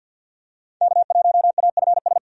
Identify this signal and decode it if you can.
Morse code, heard as a keyed tone, about 53 wpm, 700 Hz. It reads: K1AFS
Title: USA Call Signs - 5 Characters - 50wpm